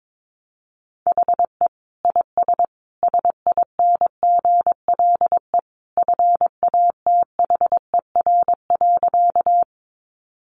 Morse code, heard as a keyed tone, about 22 wpm, 700 Hz. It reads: HE IS SINGLE FATHER.
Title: Sentences from Top 700 Words 22wpm